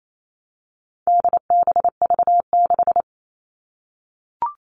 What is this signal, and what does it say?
Morse code, heard as a keyed tone, about 28 wpm, 700 Hz. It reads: DBV6
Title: All Cut Numbers - 4-Digit 28wpm